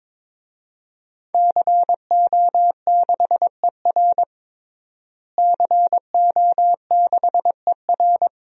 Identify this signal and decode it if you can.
Morse code, heard as a keyed tone, about 22 wpm, 700 Hz. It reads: CO6ER CO6ER
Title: DXCC-100 Call Signs - 5 Characters - Repeated 3x - 22wpm